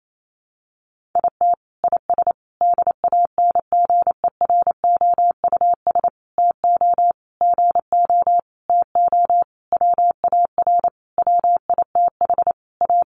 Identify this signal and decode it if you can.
Morse code, heard as a keyed tone, about 28 wpm, 700 Hz. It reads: IT IS DANGEROUS TO GO TO WAR WITH A
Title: Sentences from Top 1200 Words 28wpm